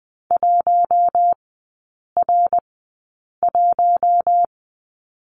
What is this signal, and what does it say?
Morse code, heard as a keyed tone, about 20 wpm, spaced as for 8 wpm, 690 Hz. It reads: 1R1
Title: 081 Lesson 14e - Number 2 - Three Characters 20wpm